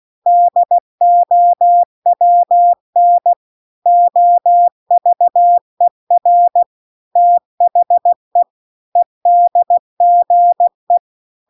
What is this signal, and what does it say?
Morse code, heard as a keyed tone, about 16 wpm, 700 Hz. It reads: DOWN OVER THE EDGE
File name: CW-The-Big-Twoheaded-River-2-16WPM